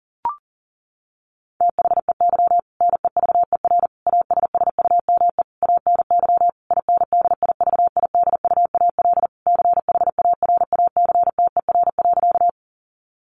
Morse code, heard as a keyed tone, about 37 wpm, 700 Hz. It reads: THEY NEVER ASSUME ANY INDIVIDUAL CHARACTER.
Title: Sentences from Top 1500 Words 40wpm